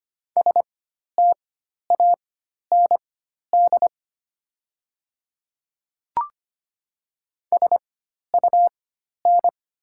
Morse code, heard as a keyed tone, about 25 wpm, spaced as for 12 wpm, 690 Hz. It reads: STAND SUN
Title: 4x Character Spacing - Top 300 Words - 25wpm